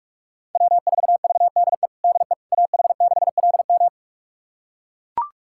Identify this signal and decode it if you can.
Morse code, heard as a keyed tone, about 42 wpm, 700 Hz. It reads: W4VDE DE AH6LM
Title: USA Callsign DE Callsign - 5 Characters - 45wpm